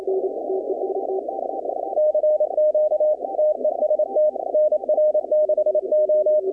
Morse code, heard as a keyed tone, about 27 wpm, 610 Hz. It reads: CQ TEST NR6O
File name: contests / 2019-cw-wpx-kt5j-k5pi- -2330 0-20